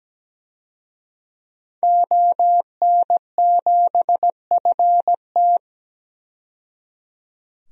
Morse code, heard as a keyed tone, about 17 wpm, 700 Hz. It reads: ON7FT
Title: DXCC-100 Call Signs - 5 Characters - 17wpm